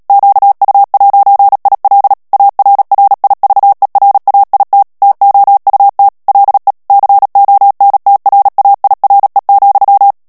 Morse code, heard as a keyed tone, about 37 wpm, 800 Hz. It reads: QU'IL ARRIVERAIT TOUT LE CONTRAIRE,
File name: CW / FICHIERS-MP3 / FRANCAIS TEXTE CLAIR 2 VITESSE 35 MOTS MINUTES